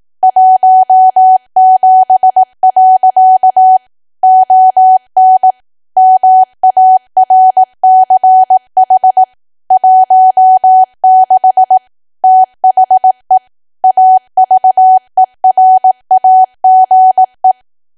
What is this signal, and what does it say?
Morse code, heard as a keyed tone, about 18 wpm, 750 Hz. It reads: 17. ON MARCH 16 THE AVERAGE
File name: cwa / W1AW-Level2 / 18WPM